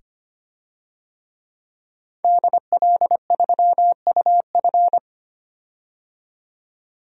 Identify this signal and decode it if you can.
Morse code, heard as a keyed tone, about 25 wpm, 700 Hz. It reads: DL3UF
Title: DXCC-100 Call Signs - 5 Characters - 25wpm